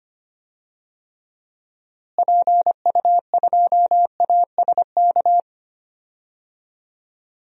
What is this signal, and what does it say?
Morse code, heard as a keyed tone, about 25 wpm, 700 Hz. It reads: PU2ASK
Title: DXCC-100 Call Signs - 6 Characters - Repeated 3x - 25wpm